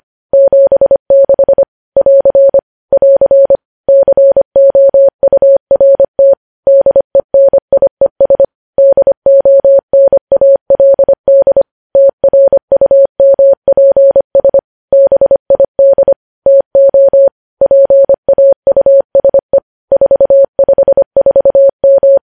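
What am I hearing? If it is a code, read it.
Morse code, heard as a keyed tone, about 25 wpm, 550 Hz. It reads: 76 <AR> <AR> COURT DENIES DONALD TRUMPS BID TO PAUSE 454M
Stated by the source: News headlines sent in Morse at 25 wpm.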